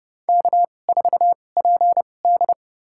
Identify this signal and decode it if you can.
Morse code, heard as a keyed tone, about 32 wpm, spaced as for 22 wpm, 700 Hz. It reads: K4PD
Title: USA Call Signs - 4 Characters - 2x Character Spacing - 30wpm